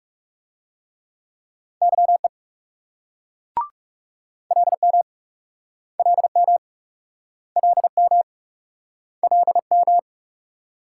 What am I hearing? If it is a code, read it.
Morse code, heard as a keyed tone, about 42 wpm, 700 Hz. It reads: YE LM LM LM LM